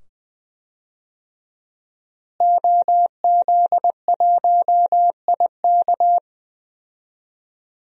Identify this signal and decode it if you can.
Morse code, heard as a keyed tone, about 20 wpm, 700 Hz. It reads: OZ1IK
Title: DXCC-100 Call Signs - 5 Characters - Repeated 2x - 20wpm